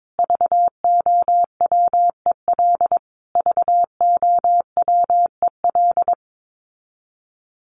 Morse code, heard as a keyed tone, about 22 wpm, 700 Hz. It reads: VOWEL VOWEL
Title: Top 500 Words Repeated 22wpm